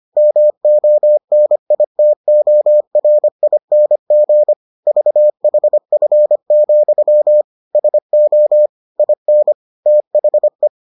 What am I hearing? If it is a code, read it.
Morse code, heard as a keyed tone, about 25 wpm, 600 Hz. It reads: MONITORING VHF, SO IN THE